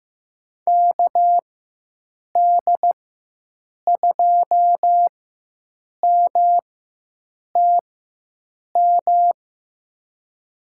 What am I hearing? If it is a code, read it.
Morse code, heard as a keyed tone, about 15 wpm, spaced as for 7 wpm, 700 Hz. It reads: KD2MTM
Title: USA Call Signs - 6 Characters - 4x Character Spacing - 15wpm